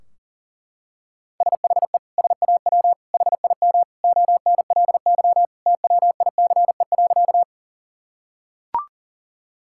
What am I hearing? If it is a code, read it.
Morse code, heard as a keyed tone, about 40 wpm, 700 Hz. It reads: SHE SAW HIM ONLY TWICE.
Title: Sentences from Top 1600 Words 40wpm